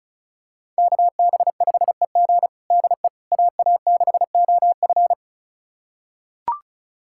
Morse code, heard as a keyed tone, about 36 wpm, 700 Hz. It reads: KB5EG DE AA6OF
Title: USA Callsign DE Callsign - 5 Characters - 35wpm